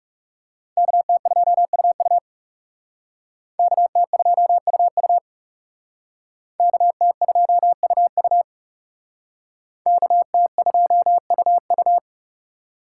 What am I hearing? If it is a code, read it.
Morse code, heard as a keyed tone, about 37 wpm, 700 Hz. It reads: KT2UU KT2UU KT2UU KT2UU
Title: Speed Racing - USA Call Signs 30wpm